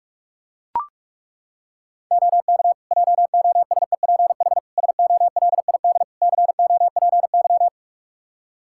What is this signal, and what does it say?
Morse code, heard as a keyed tone, about 43 wpm, 700 Hz. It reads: OK JOSEPH SOLID COPY
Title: Practice QSO - Exchange 3 - More Info - 45wpm